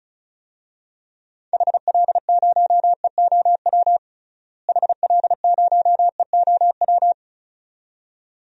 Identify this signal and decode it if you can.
Morse code, heard as a keyed tone, about 36 wpm, 700 Hz. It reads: HL0EOW HL0EOW
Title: DXCC-100 Call Signs - 6 Characters - Repeated 2x - 35wpm